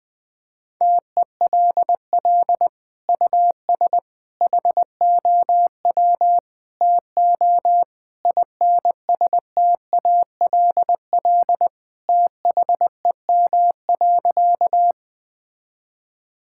Morse code, heard as a keyed tone, about 20 wpm, 700 Hz. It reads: TELL US HOW TO INSTALL THEM.
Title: Sentences from Top 1600 Words 20wpm